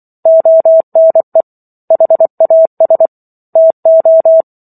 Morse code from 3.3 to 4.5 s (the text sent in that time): TO